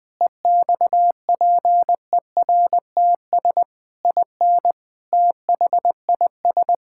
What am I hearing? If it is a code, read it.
Morse code, heard as a keyed tone, about 20 wpm, 700 Hz. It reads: EXPERTS IN THIS
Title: Sentences with 5-Word Phrases - Spoken Beginning - 20wpm